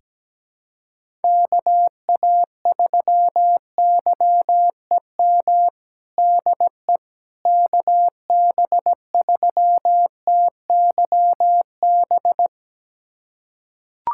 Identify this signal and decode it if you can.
Morse code, heard as a keyed tone, about 17 wpm, 700 Hz. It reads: KA3YEM DE KB3TYB
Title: USA Callsign DE Callsign 17wpm